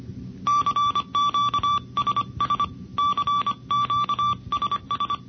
Morse code, heard as a keyed tone, about 25 wpm, 1160 Hz. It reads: CQSS CQSS
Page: NCCC SS Tape #2 - 1978 by: W1ARR, N6IG, N6NE and WA6HJV